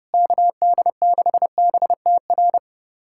Morse code, heard as a keyed tone, about 32 wpm, 700 Hz. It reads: KD6BTR
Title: USA Callsign DE Callsign 30wpm